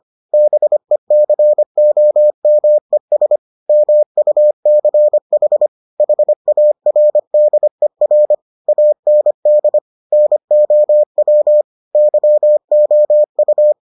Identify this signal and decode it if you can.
Morse code, heard as a keyed tone, about 25 wpm, 600 Hz. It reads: BECOMES MUCH HARDER AND NOW YOU